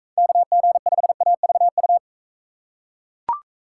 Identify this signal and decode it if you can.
Morse code, heard as a keyed tone, about 41 wpm, 700 Hz. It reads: KG5AVU